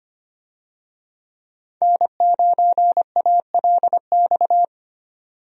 Morse code, heard as a keyed tone, about 25 wpm, 700 Hz. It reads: N9ALX
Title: USA Call Signs - 5 Characters - 25wpm